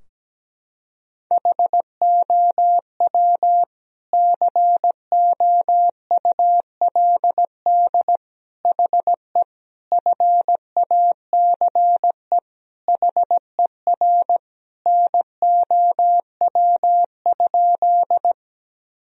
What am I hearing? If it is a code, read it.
Morse code, heard as a keyed tone, about 17 wpm, 700 Hz. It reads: HOW COULD HE FACE HER NOW?
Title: Sentences from Top 1000 Words - Review Part 2 - 17wpm